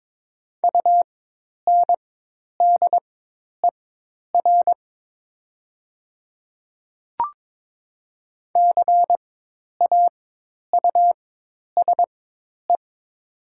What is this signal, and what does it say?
Morse code, heard as a keyed tone, about 22 wpm, spaced as for 10 wpm, 700 Hz. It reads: UNDER CAUSE
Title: 4x Character Spacing - Top 200 Words - 22wpm